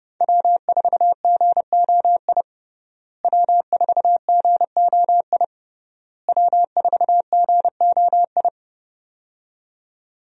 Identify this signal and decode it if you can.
Morse code, heard as a keyed tone, about 30 wpm, 700 Hz. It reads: W4GOI W4GOI W4GOI
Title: USA Call Signs - 5 Characters - Repeated 3x - 30wpm